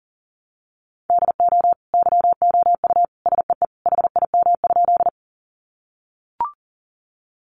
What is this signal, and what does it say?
Morse code, heard as a keyed tone, about 40 wpm, 700 Hz. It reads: DO YOU SEE HIM?
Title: Sentences from Top 100 Words 40wpm